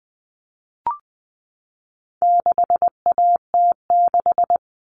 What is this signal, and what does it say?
Morse code, heard as a keyed tone, about 20 wpm, 700 Hz. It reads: 6AT6
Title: All Cut Numbers - 4-Digit 20wpm